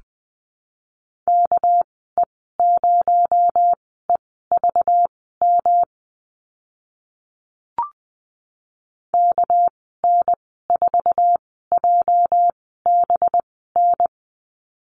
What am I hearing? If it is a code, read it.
Morse code, heard as a keyed tone, about 20 wpm, spaced as for 14 wpm, 700 Hz. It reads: KE0EVM KN4JBN
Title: 245 Lesson 39g - Letter z - Call Signs 20wpm